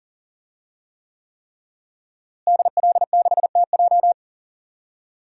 Morse code, heard as a keyed tone, about 40 wpm, 700 Hz. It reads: DL6TJ
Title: DXCC-100 Call Signs - 5 Characters - Repeated 3x - 40wpm